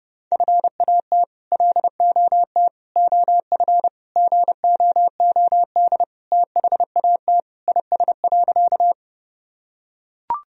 Morse code, heard as a keyed tone, about 30 wpm, 700 Hz. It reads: FAT LOT OF GOOD THAT IS.
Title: Sentences from Top 1000 Words - Review Part 4 - 30wpm